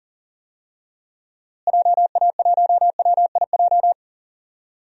Morse code, heard as a keyed tone, about 37 wpm, 700 Hz. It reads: JA1WIJ
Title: DXCC-100 Call Signs - 6 Characters - Repeated 2x - 40wpm